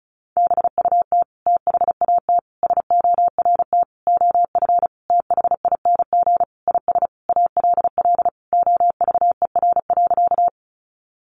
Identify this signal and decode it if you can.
Morse code, heard as a keyed tone, about 34 wpm, 700 Hz. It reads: BUT THAT SORT OF THING IS ALL OVER.
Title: Sentences from Top 1300 Words 35wpm